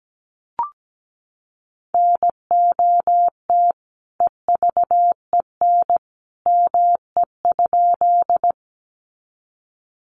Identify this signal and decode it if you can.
Morse code, heard as a keyed tone, about 17 wpm, 700 Hz. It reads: NOT EVEN ME?
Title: Sentences from Top 600 Words 17wpm